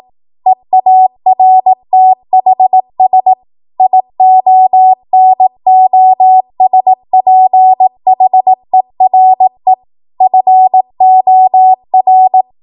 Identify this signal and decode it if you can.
Morse code, heard as a keyed tone, about 18 wpm, 750 Hz. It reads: EARTHS IONOSPHERE FOR
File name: Morse / 18WPM